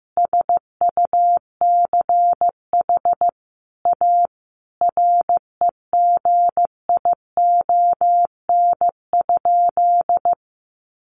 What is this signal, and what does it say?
Morse code, heard as a keyed tone, about 15 wpm, 700 Hz. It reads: SUCH A REGION?
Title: Sentences from Top 1500 Words 15wpm